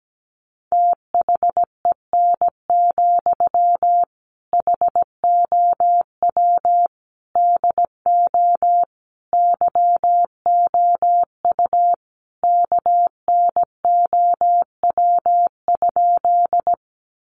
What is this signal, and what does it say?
Morse code, heard as a keyed tone, about 17 wpm, 700 Hz. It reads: THEN, HOW DO YOU KNOW?
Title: Sentences from Top 100 Words 17wpm